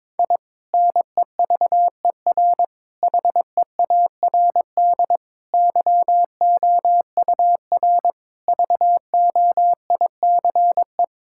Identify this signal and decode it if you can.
Morse code, heard as a keyed tone, about 21 wpm, 700 Hz. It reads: I NEVER HEARD YOUR VOICE
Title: Sentences from Top 1000 Words - Review Part 6 - 22wpm